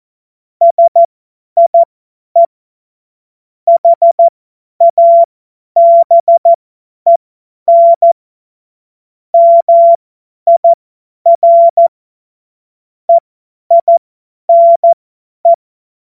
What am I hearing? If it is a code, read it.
Morse code, heard as a keyed tone, about 14 wpm, spaced as for 10 wpm, 680 Hz. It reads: SIE HABEN MIR EINE
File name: Morsekurs / Witze / Witz 13. Dezember 2022 50 F70